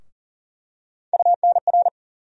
Morse code, heard as a keyed tone, about 40 wpm, 700 Hz. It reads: UNR
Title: USA Call Sign Postfixes - 3 Letters 40wpm